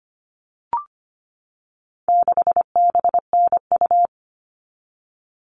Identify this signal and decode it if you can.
Morse code, heard as a keyed tone, about 25 wpm, 700 Hz. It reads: 6BNU
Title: All Cut Numbers - 4-Digit 25wpm